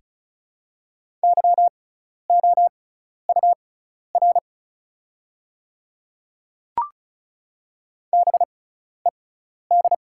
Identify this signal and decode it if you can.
Morse code, heard as a keyed tone, about 36 wpm, spaced as for 12 wpm, 700 Hz. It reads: YOUR BED
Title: 6x Character Spacing - Top 500 Words - 35wpm